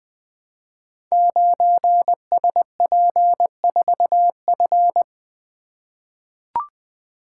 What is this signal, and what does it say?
Morse code, heard as a keyed tone, about 20 wpm, 700 Hz. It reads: 9SP4F
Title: Code Groups - 5 Letter:Numbers - 20wpm